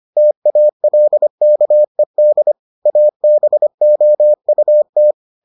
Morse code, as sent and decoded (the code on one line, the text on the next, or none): - .- .-.. -.- . -.. / .- -... --- ..- -
TALKED ABOUT